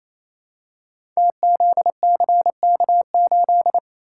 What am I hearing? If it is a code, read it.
Morse code, heard as a keyed tone, about 27 wpm, 700 Hz. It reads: TZCK8
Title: Code Groups - 5 Letter:Numbers - 28wpm